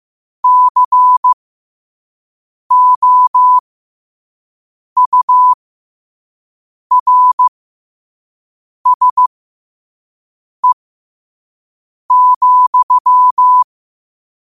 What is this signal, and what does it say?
Morse code, heard as a keyed tone, about 15 wpm, spaced as for 5 wpm, 1000 Hz. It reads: COURSE,
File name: qotdfiles / 5wpm